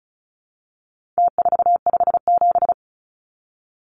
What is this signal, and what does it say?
Morse code, heard as a keyed tone, about 34 wpm, 700 Hz. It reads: T457
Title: Numbers - Contest Style - 4-Digit 35wpm